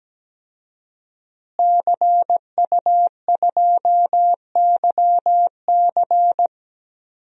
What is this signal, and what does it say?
Morse code, heard as a keyed tone, about 17 wpm, 700 Hz. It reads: CU2YC
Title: DXCC-100 Call Signs - 5 Characters - Repeated 2x - 17wpm